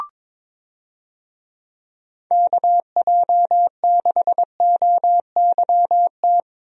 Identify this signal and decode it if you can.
Morse code, heard as a keyed tone, about 22 wpm, 700 Hz. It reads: KJ6OYT
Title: USA Call Signs - 6 Characters - Repeated 2x - 22wpm